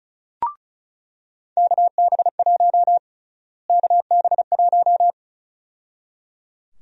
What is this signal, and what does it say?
Morse code, heard as a keyed tone, about 34 wpm, 700 Hz. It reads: KB1 KB1
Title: USA Call Sign Prefixes - Repeated 2x - 35wpm